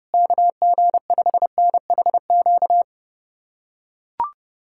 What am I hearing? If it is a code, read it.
Morse code, heard as a keyed tone, about 30 wpm, 700 Hz. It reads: KG5NHQ
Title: USA Callsign DE Callsign 30wpm